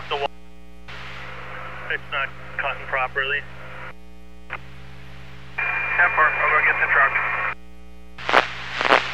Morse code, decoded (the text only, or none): none